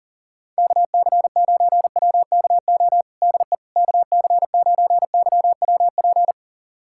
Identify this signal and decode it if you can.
Morse code, heard as a keyed tone, about 40 wpm, 700 Hz. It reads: KC9WKO DE KC9YWP
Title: USA Callsign DE Callsign 40wpm